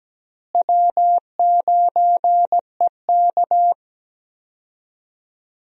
Morse code, heard as a keyed tone, about 17 wpm, 700 Hz. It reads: W9EK
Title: USA Call Signs - 4 Characters - Repeated 2x - 17wpm